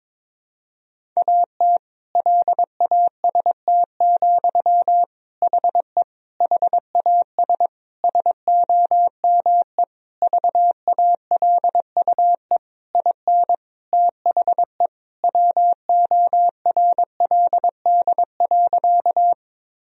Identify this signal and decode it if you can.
Morse code, heard as a keyed tone, about 22 wpm, 700 Hz. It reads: AT LAST, HE HAS SOME VALUE IN THE WORLD.
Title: Sentences from Top 1000 Words - Review Part 3 - 22wpm